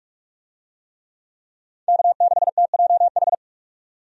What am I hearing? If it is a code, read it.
Morse code, heard as a keyed tone, about 43 wpm, 700 Hz. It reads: K6TJH
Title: USA Call Signs - 5 Characters - 45wpm